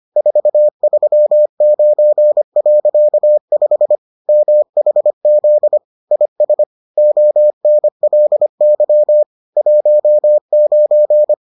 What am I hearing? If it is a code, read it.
Morse code, heard as a keyed tone, about 25 wpm, 600 Hz. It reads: 439.5 MHZ IS ONLY 19